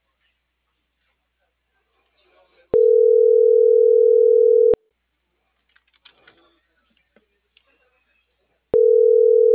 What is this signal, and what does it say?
Ringback tone (North America), 440 + 480 Hz.